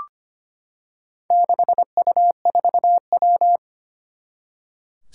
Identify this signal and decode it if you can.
Morse code, heard as a keyed tone, about 25 wpm, 700 Hz. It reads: 6U4W
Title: Four Challenging Letter-Number Combinations 25wpm